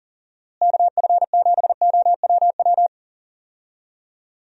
Morse code, heard as a keyed tone, about 38 wpm, 700 Hz. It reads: KF7OWW
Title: USA Call Signs - 6 Characters - Repeated 3x - 40wpm